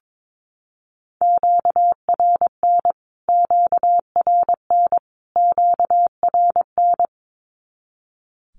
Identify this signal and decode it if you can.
Morse code, heard as a keyed tone, about 22 wpm, 700 Hz. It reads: QRN QRN QRN
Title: Q-Codes Repeated 3x 22wpm